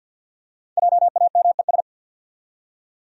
Morse code, heard as a keyed tone, about 52 wpm, 700 Hz. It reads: JAMES
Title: USA Common Names - Max 5 Letters - 50wpm